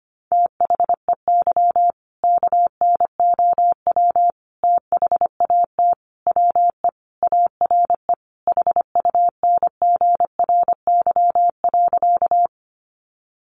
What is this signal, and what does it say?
Morse code, heard as a keyed tone, about 25 wpm, 700 Hz. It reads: THEY KNOW THAT WE ARE HUNGRY.
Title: Sentences from Top 1400 Words 25wpm